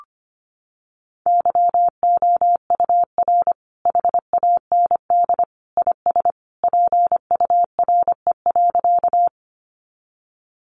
Morse code, heard as a keyed tone, about 25 wpm, 700 Hz. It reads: YOUR HAND IS PURE.
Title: Sentences from Top 1500 Words 25wpm